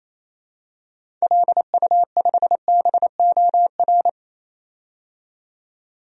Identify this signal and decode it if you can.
Morse code, heard as a keyed tone, about 28 wpm, 700 Hz. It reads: LU5BOR
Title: DXCC-100 Call Signs - 6 Characters - 28wpm